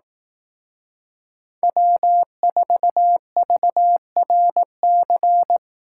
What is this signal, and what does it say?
Morse code, heard as a keyed tone, about 18 wpm, 700 Hz. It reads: W4VRC
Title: Speed Racing - USA 1x3 Call Signs 15wpm